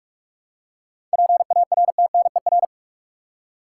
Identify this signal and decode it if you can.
Morse code, heard as a keyed tone, about 46 wpm, 700 Hz. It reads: PARTNER
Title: Seven Letter Words - 45wpm